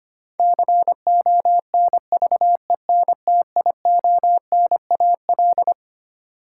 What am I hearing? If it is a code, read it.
Morse code, heard as a keyed tone, about 25 wpm, 700 Hz. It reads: CONVENTIONAL
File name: Twelve Letter Words - 25wpm